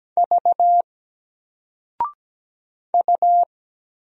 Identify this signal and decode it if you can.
Morse code, heard as a keyed tone, about 17 wpm, 700 Hz. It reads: V U
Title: Single Letter-Number 17wpm